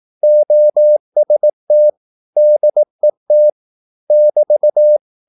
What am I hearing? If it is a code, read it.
Morse code, heard as a keyed tone, about 18 wpm, 600 Hz. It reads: OST DET <BT>
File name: Korte ord DK 18wpm